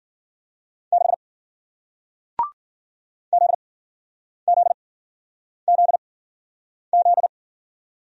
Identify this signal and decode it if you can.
Morse code, heard as a keyed tone, about 53 wpm, spaced as for 9 wpm, 700 Hz. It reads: 6 ZZZZ